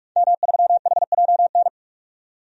Morse code, heard as a keyed tone, about 46 wpm, 700 Hz. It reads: M3HJN
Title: DXCC-100 Call Signs - 5 Characters - 45wpm